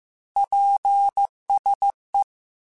Morse code, heard as a keyed tone, about 15 wpm, 800 Hz. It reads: PSE
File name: QSOs / QSOs-mp3s / WA7QOX 2@13